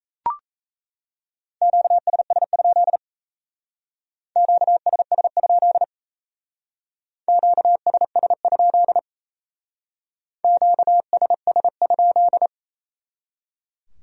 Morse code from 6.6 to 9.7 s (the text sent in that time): QSS?